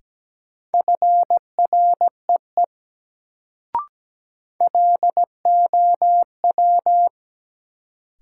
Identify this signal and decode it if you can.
Morse code, heard as a keyed tone, about 17 wpm, 700 Hz. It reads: FREE LOW
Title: Sentences from Top 500 Words - New Word Drill - 17wpm